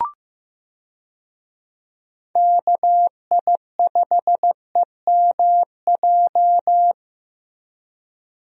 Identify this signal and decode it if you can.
Morse code, heard as a keyed tone, about 15 wpm, 700 Hz. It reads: KI5EMJ
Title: USA Call Signs - 6 Characters - 15wpm